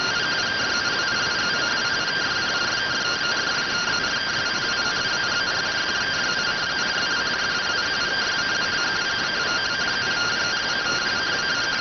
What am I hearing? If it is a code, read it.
Two-tone teleprinter signal, about 170 Hz shift.